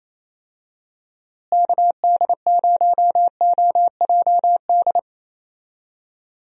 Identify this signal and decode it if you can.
Morse code, heard as a keyed tone, about 28 wpm, 700 Hz. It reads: KD0OJD
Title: USA Call Signs - 6 Characters - Repeated 2x - 28wpm